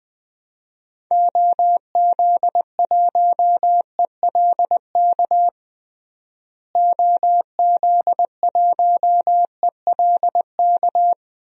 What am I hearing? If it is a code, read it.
Morse code, heard as a keyed tone, about 21 wpm, 700 Hz. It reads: OZ1ELK OZ1ELK
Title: DXCC-100 Call Signs - 6 Characters - Repeated 2x - 20wpm